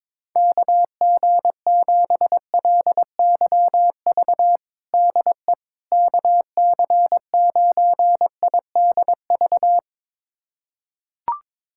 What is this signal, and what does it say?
Morse code, heard as a keyed tone, about 22 wpm, 700 Hz. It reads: KG7LYV DE KC9IDV
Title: USA Callsign DE Callsign - 6 Characters - 22wpm